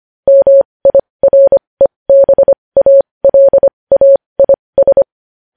Morse code, heard as a keyed tone, about 25 wpm, spaced as for 20 wpm, 550 Hz. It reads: MIREBALAIS
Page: news-headlines-in-morse-code-at-20-wpm